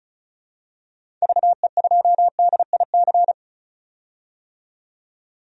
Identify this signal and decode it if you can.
Morse code, heard as a keyed tone, about 36 wpm, 700 Hz. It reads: VE2DIC
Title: DXCC-100 Call Signs - 6 Characters - 35wpm